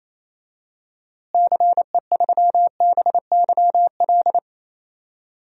Morse code, heard as a keyed tone, about 28 wpm, 700 Hz. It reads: CE3BYL
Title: DXCC-100 Call Signs - 6 Characters - Repeated 3x - 28wpm